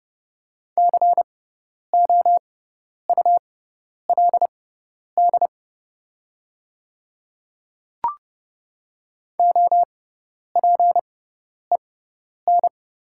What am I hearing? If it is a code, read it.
Morse code, heard as a keyed tone, about 30 wpm, spaced as for 10 wpm, 700 Hz. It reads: COULD OPEN
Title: 6x Character Spacing - Top 500 Words - 30wpm